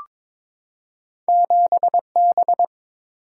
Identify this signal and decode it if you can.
Morse code, heard as a keyed tone, about 22 wpm, 700 Hz. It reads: 7B
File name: Two Letter-Number Combinations - 22wpm